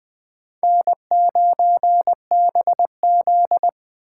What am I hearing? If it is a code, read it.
Morse code, heard as a keyed tone, about 21 wpm, 700 Hz. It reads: N9BZ
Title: Speed Racing - USA 1x2 Call Signs 17wpm